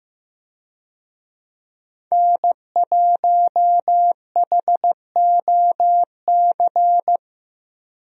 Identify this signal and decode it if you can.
Morse code, heard as a keyed tone, about 15 wpm, 700 Hz. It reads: N1HOC
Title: USA Call Signs - 5 Characters - 15wpm